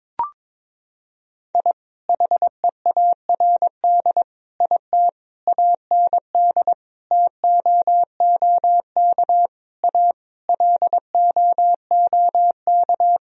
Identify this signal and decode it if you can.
Morse code, heard as a keyed tone, about 22 wpm, 700 Hz. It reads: I HEARD IT AND TOOK A LOOK
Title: Sentences from Top 1000 Words - Review Part 1 - 22wpm